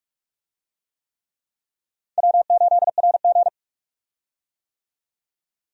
Morse code, heard as a keyed tone, about 46 wpm, 700 Hz. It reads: W8RG
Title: USA Call Signs - 4 Characters - 45wpm